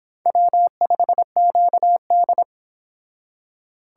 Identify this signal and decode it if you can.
Morse code, heard as a keyed tone, about 25 wpm, 700 Hz. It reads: W5QD